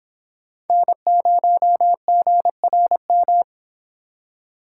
Morse code, heard as a keyed tone, about 26 wpm, 700 Hz. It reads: N0GRM